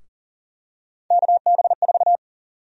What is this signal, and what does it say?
Morse code, heard as a keyed tone, about 37 wpm, 700 Hz. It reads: KB4
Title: USA Call Sign Prefixes - 40wpm